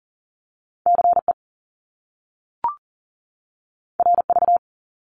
Morse code, heard as a keyed tone, about 42 wpm, 700 Hz. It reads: CE RV
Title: Top Two Letter Combinations in Words 40wpm